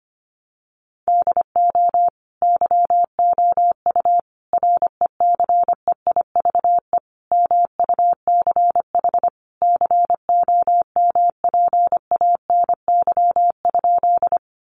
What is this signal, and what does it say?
Morse code, heard as a keyed tone, about 25 wpm, 700 Hz. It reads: DO YOU RECEIVE MUCH COMPANY?
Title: Sentences from Top 1000 Words 25wpm